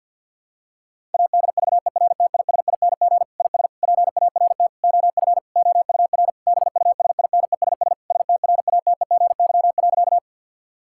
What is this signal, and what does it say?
Morse code, heard as a keyed tone, about 52 wpm, 700 Hz. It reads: ADVERTISING IS PART OF OUR BUSINESS STRATEGY.
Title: Sentences from Top 1000 Words - Review Part 6 - 50wpm